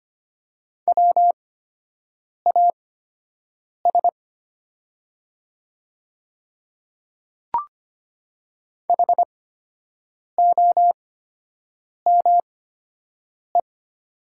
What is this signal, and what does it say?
Morse code, heard as a keyed tone, about 25 wpm, spaced as for 7 wpm, 700 Hz. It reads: WAS HOME
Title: 8x Character Spacing - Top 200 Words - 25wpm